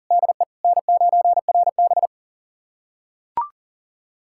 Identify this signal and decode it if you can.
Morse code, heard as a keyed tone, about 38 wpm, 700 Hz. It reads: DE N9RB
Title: USA Callsign DE Callsign - 4 Characters - 40wpm